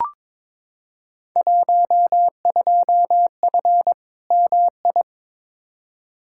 Morse code, heard as a keyed tone, about 22 wpm, 700 Hz. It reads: 12F MI
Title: Field Day Exchanges Encore - 22wpm